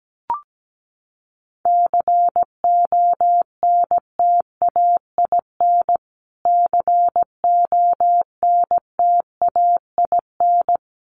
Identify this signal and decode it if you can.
Morse code, heard as a keyed tone, about 17 wpm, 700 Hz. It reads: CONTAIN CONTAIN
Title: Top 500 Words Repeated 17wpm